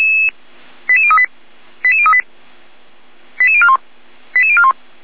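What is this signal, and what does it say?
Five-tone selective call (ZVEI), heard as quick runs of five tones, tones of about 70 ms. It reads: 88038 88038 88041 88041
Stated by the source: Received on an SDR.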